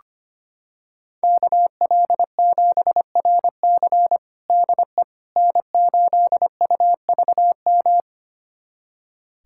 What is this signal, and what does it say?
Morse code, heard as a keyed tone, about 25 wpm, 700 Hz. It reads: KL7RC DE N8UVM
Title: USA Callsign DE Callsign - 5 Characters - 25wpm